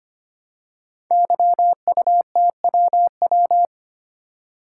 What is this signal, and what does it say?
Morse code, heard as a keyed tone, about 25 wpm, 700 Hz. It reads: YUTWW
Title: Code Groups - 5 Letters - 25wpm